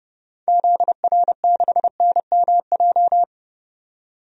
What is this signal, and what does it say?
Morse code, heard as a keyed tone, about 31 wpm, 700 Hz. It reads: ZR6NMJ
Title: DXCC-100 Call Signs - 6 Characters - 30wpm